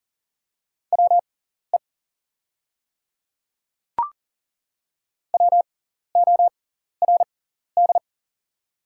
Morse code, heard as a keyed tone, about 40 wpm, spaced as for 14 wpm, 690 Hz. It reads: WE WORD
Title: 6x Character Spacing - Top 100 Words - 40wpm